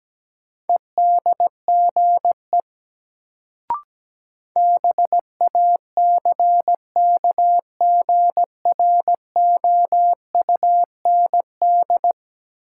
Morse code, heard as a keyed tone, about 17 wpm, 700 Hz. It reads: EDGE BACKGROUND
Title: Sentences from Top 1300 Words - New Word Drill - 17wpm